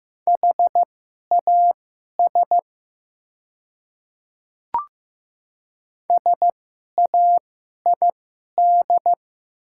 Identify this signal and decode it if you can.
Morse code, heard as a keyed tone, about 15 wpm, spaced as for 11 wpm, 700 Hz. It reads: HAS SAID
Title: 2x Character Spacing - Top 300 Words - 15wpm